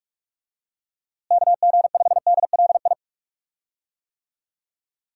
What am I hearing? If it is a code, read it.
Morse code, heard as a keyed tone, about 46 wpm, 700 Hz. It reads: KG5DLI
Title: USA Call Signs - 6 Characters - Repeated 3x - 45wpm